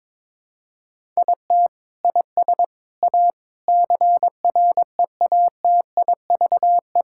Morse code, heard as a keyed tone, about 22 wpm, 700 Hz. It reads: IT IS A CREATIVE
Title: Sentences from Top 1000 Words - Review Part 5 - 22wpm